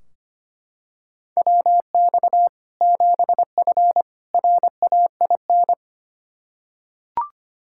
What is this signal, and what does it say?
Morse code, heard as a keyed tone, about 25 wpm, 700 Hz. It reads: WX 7F RAIN
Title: Practice QSO - Exchange 2 - Info - 25wpm